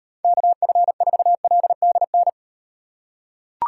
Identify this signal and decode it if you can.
Morse code, heard as a keyed tone, about 36 wpm, 700 Hz. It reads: KF4LDN